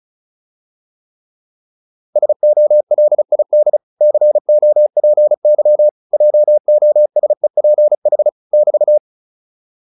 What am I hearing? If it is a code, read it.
Morse code, heard as a keyed tone, about 34 wpm, 600 Hz. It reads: SOLID COPY JOSEPH <BT>